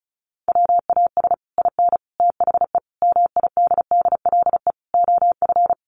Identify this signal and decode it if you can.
Morse code, heard as a keyed tone, about 34 wpm, 700 Hz. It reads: WAS IN THE MIDDLE OF
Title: Sets of 5 Words Encore 35wpm